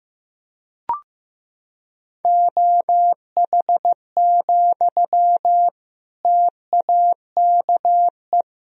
Morse code, heard as a keyed tone, about 16 wpm, 700 Hz. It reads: OH, TAKE
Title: Sentences from Top 1000 Words - Review Part 1 - 15wpm